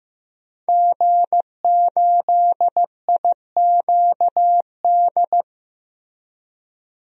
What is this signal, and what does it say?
Morse code, heard as a keyed tone, about 15 wpm, 700 Hz. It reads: G8IQD
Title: DXCC-100 Call Signs - 5 Characters - Repeated 2x - 15wpm